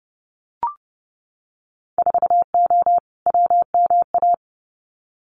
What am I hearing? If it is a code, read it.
Morse code, heard as a keyed tone, about 30 wpm, 700 Hz. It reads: 4O WMA
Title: Winter Field Day Exchanges Encore 30wpm